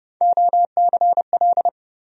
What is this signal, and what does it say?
Morse code, heard as a keyed tone, about 32 wpm, 700 Hz. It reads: OCL
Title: USA Call Sign Postfixes - 3 Letters 30wpm